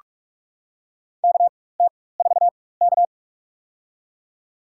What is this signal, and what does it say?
Morse code, heard as a keyed tone, about 45 wpm, spaced as for 21 wpm, 700 Hz. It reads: KT4K
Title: USA Call Signs - 4x Character Spacing - 45wpm